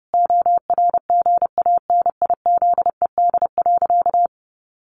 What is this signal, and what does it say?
Morse code, heard as a keyed tone, about 30 wpm, 700 Hz. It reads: ORGANIZED.
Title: Sentences from Top 1300 Words 30wpm